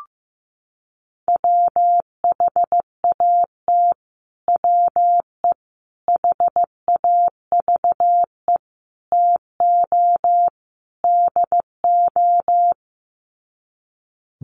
Morse code, heard as a keyed tone, about 15 wpm, 700 Hz. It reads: WHAT WE HAVE TO DO
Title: Sets of 5 Words Encore 15wpm